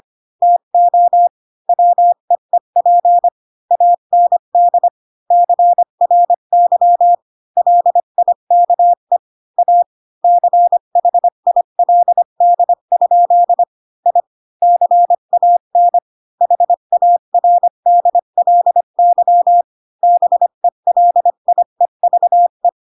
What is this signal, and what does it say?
Morse code, heard as a keyed tone, about 25 wpm, 700 Hz. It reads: TO WEEP AND CRY LIKE A CHILD? I CAN HARDLY BELIEVE
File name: Pinocchio / Pinocchio ch1 20 25 ik1hge